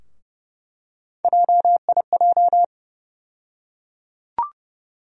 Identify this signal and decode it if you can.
Morse code, heard as a keyed tone, about 31 wpm, 700 Hz. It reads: JIJ
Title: USA Call Sign Postfixes - 3 Letters 30wpm